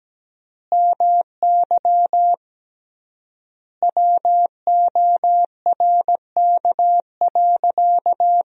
Morse code, heard as a keyed tone, about 17 wpm, 700 Hz. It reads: MY WORK.
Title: 3x Word-Spacing - Sentences from Top 300 Words - 17wpm